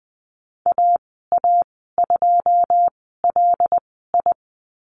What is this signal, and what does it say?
Morse code, heard as a keyed tone, about 20 wpm, spaced as for 15 wpm, 700 Hz. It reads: AA2LI
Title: 167 Lesson 27g - Number 7 - Call Signs 20wpm